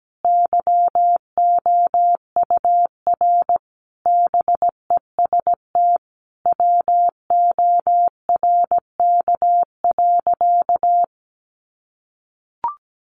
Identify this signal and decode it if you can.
Morse code, heard as a keyed tone, about 17 wpm, 700 Hz. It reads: YOUR BEST WORK.
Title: Sentences from Top 1200 Words 17wpm